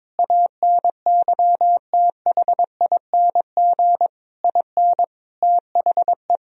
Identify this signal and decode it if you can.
Morse code, heard as a keyed tone, about 22 wpm, 700 Hz. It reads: ANYTHING IN THE
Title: Sentences from Top 1000 Words - Review Part 4 - 22wpm